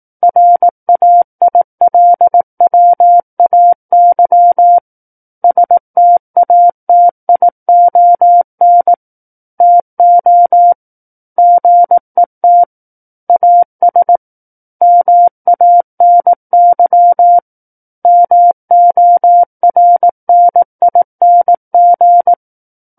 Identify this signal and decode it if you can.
Morse code, heard as a keyed tone, about 18 wpm, 700 Hz. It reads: RAILWAY STATION TO GET AS MANY MORNING
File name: 18 WPM